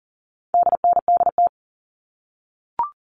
Morse code, heard as a keyed tone, about 40 wpm, 700 Hz. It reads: DNDT
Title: All Cut Numbers - 4-Digit 40wpm